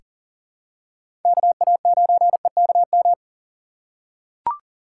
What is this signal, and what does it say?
Morse code, heard as a keyed tone, about 39 wpm, 700 Hz. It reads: KA9EKM
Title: USA Call Signs - 2x Character Spacing - 40wpm